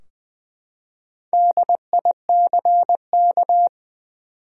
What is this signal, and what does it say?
Morse code, heard as a keyed tone, about 20 wpm, 700 Hz. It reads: DICK
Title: USA Common Names - Max 5 Letters - 20wpm